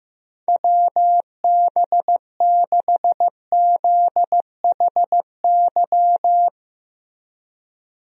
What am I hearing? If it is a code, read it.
Morse code, heard as a keyed tone, about 15 wpm, 700 Hz. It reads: WB6ZHY
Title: USA Call Signs - 6 Characters - Repeated 3x - 15wpm